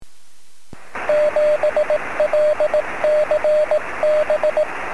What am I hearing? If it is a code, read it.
Morse code, heard as a keyed tone, about 18 wpm, 620 Hz, with radio noise. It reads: 7LCB
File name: sm7lcb / 02-sm0fzh 23cm